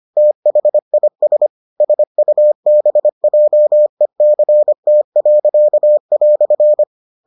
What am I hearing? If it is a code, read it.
Morse code, heard as a keyed tone, about 25 wpm, 600 Hz. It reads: THIS SUBJECT."